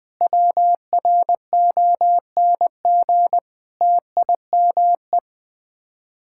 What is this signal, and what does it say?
Morse code, heard as a keyed tone, about 21 wpm, 700 Hz. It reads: WRONG TIME
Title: Sets of 9 Words - 20wpm